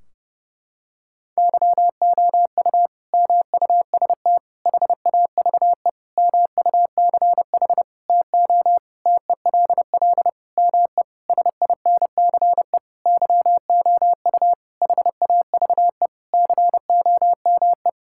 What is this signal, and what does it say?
Morse code, heard as a keyed tone, about 30 wpm, 700 Hz. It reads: YOU MUST HAVE MUCH TO TELL ME SINCE YOU HAVE COME
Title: Sentences from Top 1600 Words 30wpm